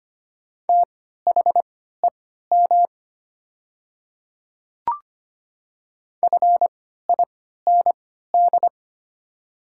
Morse code, heard as a keyed tone, about 25 wpm, spaced as for 14 wpm, 700 Hz. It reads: THEM FIND
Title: 3x Character Spacing - Top 200 Words - 25wpm